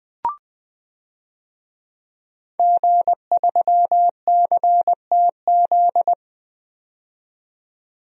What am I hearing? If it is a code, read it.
Morse code, heard as a keyed tone, about 20 wpm, 700 Hz. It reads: G3CTZ
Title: DXCC-100 Call Signs - 5 Characters - 20wpm